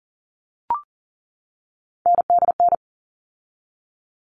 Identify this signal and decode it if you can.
Morse code, heard as a keyed tone, about 40 wpm, 700 Hz. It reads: NDN
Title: All Cut Numbers - 3-Digit 40wpm